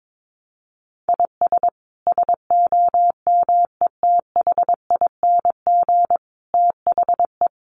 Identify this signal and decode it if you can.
Morse code, heard as a keyed tone, about 22 wpm, 700 Hz. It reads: IS SOMETHING THE
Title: Sentences from Top 700 Words 22wpm